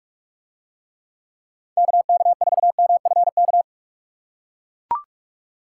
Morse code, heard as a keyed tone, about 41 wpm, 700 Hz. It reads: KK4MFK